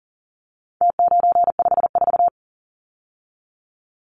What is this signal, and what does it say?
Morse code, heard as a keyed tone, about 39 wpm, 700 Hz. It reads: T954
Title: Numbers - Contest Style - 4-Digit 40wpm